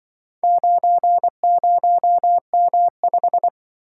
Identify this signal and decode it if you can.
Morse code, heard as a keyed tone, about 24 wpm, 700 Hz. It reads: 90M5
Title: Speed-Racing - Four Letter-Number Combinations - 20wpm